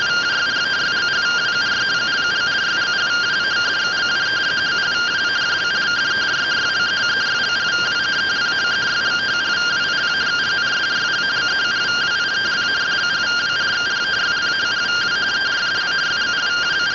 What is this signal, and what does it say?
Two-tone teleprinter signal, about 170 Hz shift.
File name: findings / CT 1 Scout 0001.gen-19 rtty rx